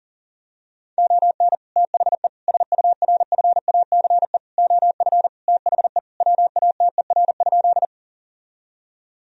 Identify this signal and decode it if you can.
Morse code, heard as a keyed tone, about 40 wpm, 700 Hz. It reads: ON THE SURFACE OF THE WATER?
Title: Sentences from Top 1000 Words - Review Part 3 - 40wpm